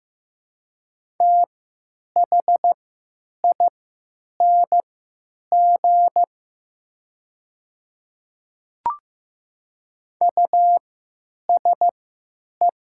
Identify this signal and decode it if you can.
Morse code, heard as a keyed tone, about 15 wpm, spaced as for 9 wpm, 700 Hz. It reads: THING USE
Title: 3x Character Spacing - Top 100 Words - 15wpm